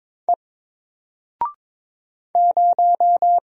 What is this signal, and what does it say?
Morse code, heard as a keyed tone, about 22 wpm, 700 Hz. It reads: E 0
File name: Single-Letter-Number-22wpm